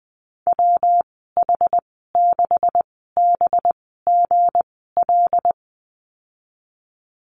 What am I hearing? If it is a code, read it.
Morse code, heard as a keyed tone, about 20 wpm, spaced as for 14 wpm, 700 Hz. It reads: WH6BGL
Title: 219 Lesson 35g - Number 0 - Call Signs 20wpm